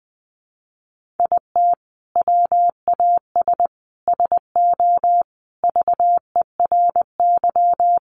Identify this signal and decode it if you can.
Morse code, heard as a keyed tone, about 20 wpm, 700 Hz. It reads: IT WAS SO VERY
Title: Sentences from Top 1400 Words 20wpm